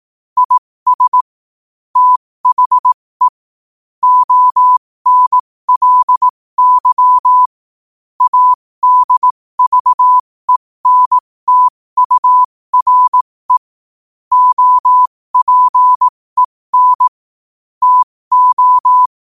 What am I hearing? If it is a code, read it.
Morse code, heard as a keyed tone, about 18 wpm, spaced as for 15 wpm, 1000 Hz. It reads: IS THE ONLY ADVENTURE OPEN TO
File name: qotdfiles / 15wpm